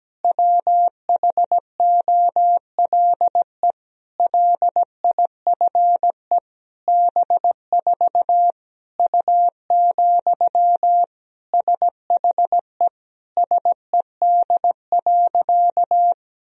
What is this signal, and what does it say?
Morse code, heard as a keyed tone, about 17 wpm, 700 Hz. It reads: WHOLE LIFE B4 U, SHE SED.
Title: Sentences from Top 400 Words and Abbreviations 17wpm